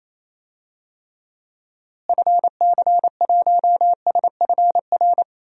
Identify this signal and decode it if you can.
Morse code, heard as a keyed tone, about 28 wpm, 700 Hz. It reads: FC1SFR
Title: DXCC-100 Call Signs - 6 Characters - Repeated 3x - 28wpm